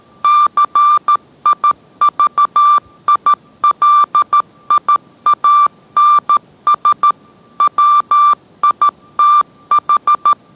Morse code, heard as a keyed tone, about 13 wpm, 1180 Hz. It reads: CIVILIANS WITH